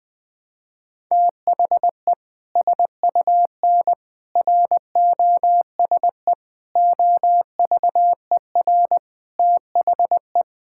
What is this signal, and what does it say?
Morse code, heard as a keyed tone, about 20 wpm, 700 Hz. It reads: THE SUN ROSE OVER THE
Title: Sentences with 2-Word Phrases - 20wpm